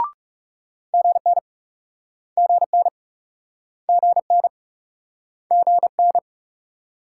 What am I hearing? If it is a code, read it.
Morse code, heard as a keyed tone, about 38 wpm, 700 Hz. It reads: GN GN GN GN